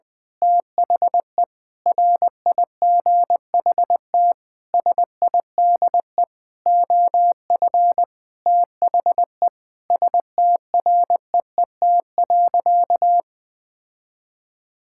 Morse code, heard as a keyed tone, about 21 wpm, 700 Hz. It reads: THE RIGHT SIDE OF THE STREET.
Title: Sentences with 6-Word Phrases - 20wpm